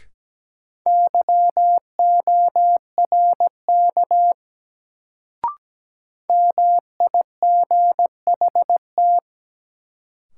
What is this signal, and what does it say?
Morse code, heard as a keyed tone, about 17 wpm, 700 Hz. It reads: YORK MIGHT
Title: Sets of 2 Words - New Word Drill - 17wpm